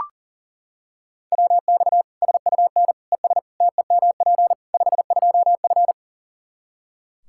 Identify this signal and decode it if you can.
Morse code, heard as a keyed tone, about 40 wpm, 700 Hz. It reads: WX SUN ES TEMP 52F
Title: Practice QSO - Exchange 2 - Info - 40wpm